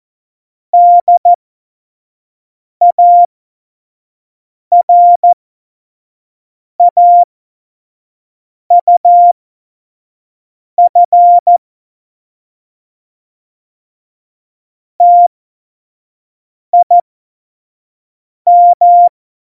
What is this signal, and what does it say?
Morse code, heard as a keyed tone, about 14 wpm, spaced as for 5 wpm, 700 Hz. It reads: DARAUF TIM
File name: Morsekurs / Witze / Witz 07. Mai 2019, 25 F70